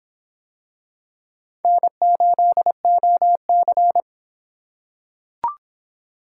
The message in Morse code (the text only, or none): N8OC